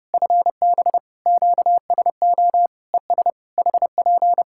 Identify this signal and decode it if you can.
Morse code, heard as a keyed tone, about 30 wpm, 700 Hz. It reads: FB QSO ES HP
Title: Practice QSO - Exchange 4 - Ending - 30wpm